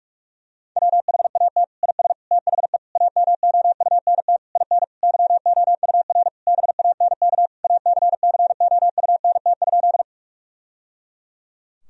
Morse code, heard as a keyed tone, about 43 wpm, 700 Hz. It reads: WHAT IS THE AMOUNT IN YOUR BANK ACCOUNT?
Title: Sentences from Top 1000 Words - Review Part 4 - 45wpm